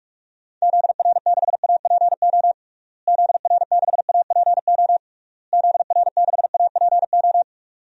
Morse code, heard as a keyed tone, about 46 wpm, 700 Hz. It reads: ZR6APO ZR6APO ZR6APO
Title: DXCC-100 Call Signs - 6 Characters - Repeated 3x - 45wpm